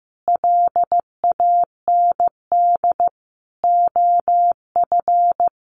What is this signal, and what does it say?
Morse code, heard as a keyed tone, about 15 wpm, 700 Hz. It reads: LAND OF
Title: Sentences from Top 1100 Words 15wpm